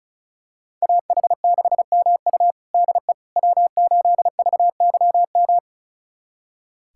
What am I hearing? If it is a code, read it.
Morse code, heard as a keyed tone, about 36 wpm, 700 Hz. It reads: AH6MU DE W8VYM
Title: USA Callsign DE Callsign - 5 Characters - 35wpm